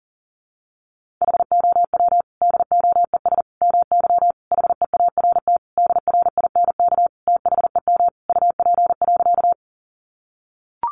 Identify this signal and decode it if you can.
Morse code, heard as a keyed tone, about 40 wpm, 700 Hz. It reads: HOW DOES MY HEART DRINK THEM UP.
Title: Sentences from Top 1500 Words 40wpm